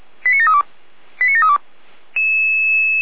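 Five-tone selective call (ZVEI), heard as quick runs of five tones, tones of about 70 ms. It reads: 89732 89732
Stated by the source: Received by SDR; ZVEI five-tone selective call.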